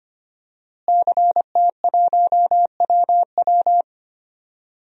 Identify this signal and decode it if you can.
Morse code, heard as a keyed tone, about 25 wpm, 700 Hz. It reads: CT1WW
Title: DXCC-100 Call Signs - 5 Characters - Repeated 3x - 25wpm